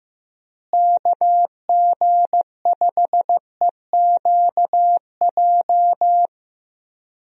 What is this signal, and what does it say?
Morse code, heard as a keyed tone, about 15 wpm, 700 Hz. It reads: KG5EQJ
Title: USA Call Signs - 6 Characters - Repeated 3x - 15wpm